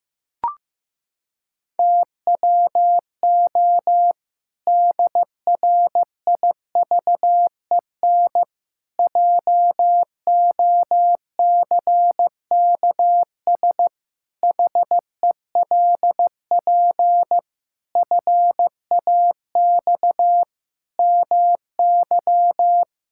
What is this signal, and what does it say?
Morse code, heard as a keyed tone, about 16 wpm, 700 Hz. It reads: TWO DRIVEN JOCKS HELP FAX MY
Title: Pangram Challenge v2 - 15wpm